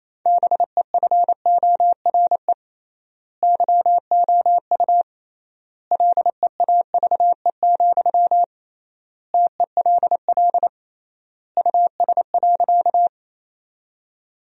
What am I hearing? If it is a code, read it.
Morse code, heard as a keyed tone, about 28 wpm, 700 Hz. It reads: BEFORE YOU LEAVE, TELL US.
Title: 3x Word-Spacing - Sentences from Top 400 Words - 28wpm